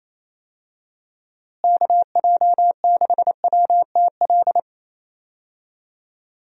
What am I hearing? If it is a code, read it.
Morse code, heard as a keyed tone, about 28 wpm, 700 Hz. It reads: KJ6WTL
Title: USA Call Signs - 6 Characters - Repeated 3x - 28wpm